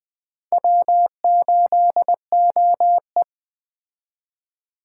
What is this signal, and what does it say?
Morse code, heard as a keyed tone, about 20 wpm, 700 Hz. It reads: W8OE